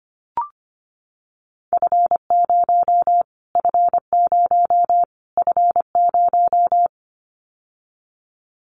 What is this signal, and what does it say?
Morse code, heard as a keyed tone, about 25 wpm, 700 Hz. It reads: F0 F0 F0
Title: Callsign Prefixes for Top 25 DXCC - Repeated 3x - 25wpm